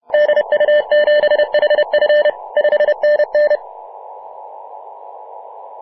Morse code, heard as a keyed tone, about 29 wpm, 600 Hz. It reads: DU7HF 5NN
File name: Mar/20 2349z FW5JJ 28.011MHz CW